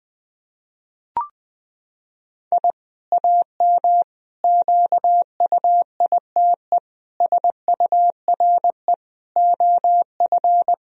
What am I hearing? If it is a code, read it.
Morse code, heard as a keyed tone, about 20 wpm, 700 Hz. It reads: I AM QUITE SURE OF
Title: Sentences from Top 1000 Words - Review Part 2 - 20wpm